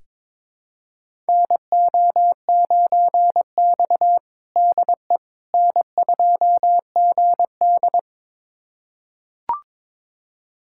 Morse code, heard as a keyed tone, about 22 wpm, 700 Hz. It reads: NO9X DE N2GD
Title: USA Callsign DE Callsign - 4 Characters - 22wpm